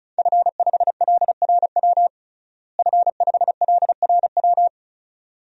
Morse code, heard as a keyed tone, about 36 wpm, 700 Hz. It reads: F5LRW F5LRW
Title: DXCC-100 Call Signs - 5 Characters - Repeated 2x - 35wpm